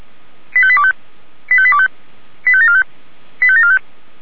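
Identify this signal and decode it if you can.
Five-tone selective call (ZVEI), heard as quick runs of five tones, tones of about 70 ms. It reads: 85825 85825 85745 85745
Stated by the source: ZVEI five-tone selective call received on an SDR by the fire brigade (Feuerwehr) Orlamünde.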